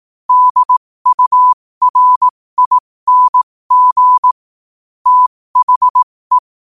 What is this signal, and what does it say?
Morse code, heard as a keyed tone, about 18 wpm, spaced as for 15 wpm, 1000 Hz. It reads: DURING THE
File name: qotdfiles / 15wpm